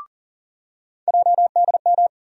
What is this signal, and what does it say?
Morse code, heard as a keyed tone, about 40 wpm, 700 Hz. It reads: JDM
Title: USA Call Sign Postfixes - 3 Letters 40wpm